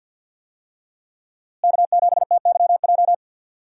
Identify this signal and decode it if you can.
Morse code, heard as a keyed tone, about 51 wpm, 700 Hz. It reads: K7TYJ
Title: USA Call Signs - 5 Characters - Repeated 2x - 50wpm